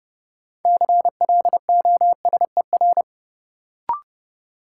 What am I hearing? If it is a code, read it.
Morse code, heard as a keyed tone, about 30 wpm, 700 Hz. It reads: CLOSER
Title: Sets of 5 Words Encore - New Word Drill - 30wpm